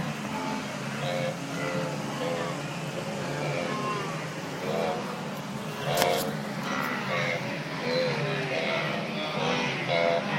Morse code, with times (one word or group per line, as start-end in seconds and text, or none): none